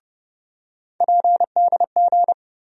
Morse code, heard as a keyed tone, about 30 wpm, 700 Hz. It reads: PDG
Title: USA Call Sign Postfixes - 3 Letters 30wpm